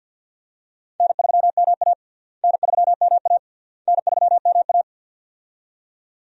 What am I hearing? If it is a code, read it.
Morse code, heard as a keyed tone, about 48 wpm, 700 Hz. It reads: N3MA N3MA N3MA
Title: USA Call Signs - 4 Characters - Repeated 3x - 50wpm